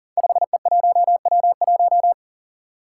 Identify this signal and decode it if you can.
Morse code, heard as a keyed tone, about 43 wpm, 700 Hz. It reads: 5E1W1
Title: Five Challenging Letter-Number Combinations 40wpm